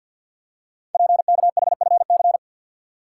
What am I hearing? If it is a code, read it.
Morse code, heard as a keyed tone, about 52 wpm, 700 Hz. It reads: PKHFC
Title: Code Groups - 5 Letters - 50wpm